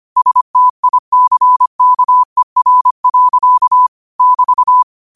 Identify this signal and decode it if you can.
Morse code, heard as a keyed tone, about 25 wpm, 1000 Hz. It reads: STICKER. <BT>
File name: qotdfiles / 25wpm